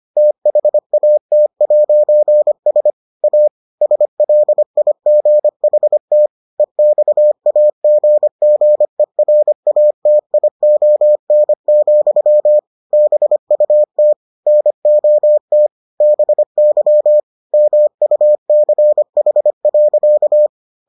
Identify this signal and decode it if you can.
Morse code, heard as a keyed tone, about 25 wpm, 600 Hz. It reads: THAT'S A SLIGHT EXAGGERATION, BUT NOT BY MUCH.